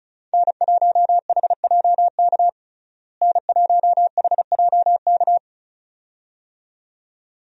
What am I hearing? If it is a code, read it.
Morse code, heard as a keyed tone, about 36 wpm, 700 Hz. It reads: N1HJK N1HJK
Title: USA Call Signs - 5 Characters - Repeated 2x - 35wpm